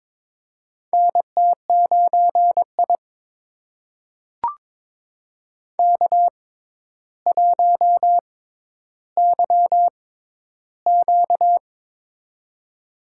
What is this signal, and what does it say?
Morse code, heard as a keyed tone, about 22 wpm, spaced as for 11 wpm, 700 Hz. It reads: NT9I K1YQ
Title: USA Call Signs - 4 Characters - 6x Character Spacing - 22wpm